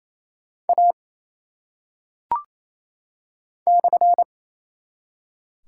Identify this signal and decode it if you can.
Morse code, heard as a keyed tone, about 27 wpm, 700 Hz. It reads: A /
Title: Top 180 QSO Elements 28wpm